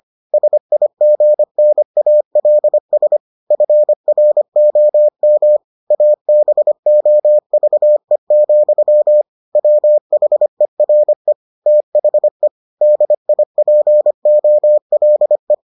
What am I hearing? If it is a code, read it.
Morse code, heard as a keyed tone, about 25 wpm, 600 Hz. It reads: SIGNALS FROM ABOVE, WHERE THE DIPOLE